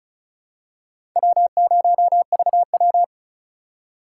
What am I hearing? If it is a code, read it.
Morse code, heard as a keyed tone, about 34 wpm, 690 Hz. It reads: W0VW
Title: USA Call Signs - 4 Characters - 2x Character Spacing - 35wpm